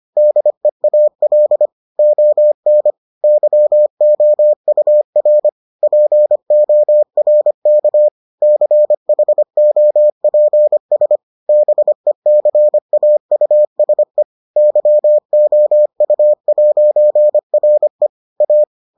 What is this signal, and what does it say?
Morse code, heard as a keyed tone, about 25 wpm, 600 Hz. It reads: DEAL ON YOUR PORK CHOPS BECAUSE YOU'RE A